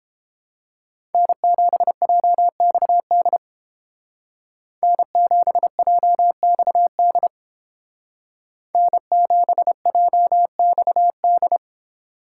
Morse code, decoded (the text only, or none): N7JXD N7JXD N7JXD